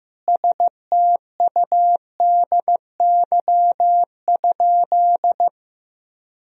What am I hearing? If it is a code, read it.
Morse code, heard as a keyed tone, about 15 wpm, 700 Hz. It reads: STUDY?
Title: Sentences from Top 1000 Words - Review Part 1 - 15wpm